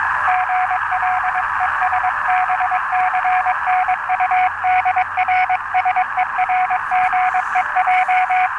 Morse code, heard as a keyed tone, about 22 wpm, 740 Hz, with radio noise. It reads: GL ES BCNU DR SERGEJ
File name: attachments / 5.03.2015 RV3GM 14060